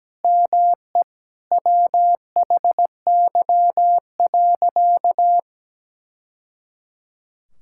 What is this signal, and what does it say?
Morse code, heard as a keyed tone, about 17 wpm, 700 Hz. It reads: ME WHY.
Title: Sentences from Top 1000 Words - Review Part 6 - 17wpm